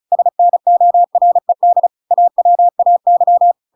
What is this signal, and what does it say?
Morse code, heard as a keyed tone, about 34 wpm, 700 Hz. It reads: SNORED AWAY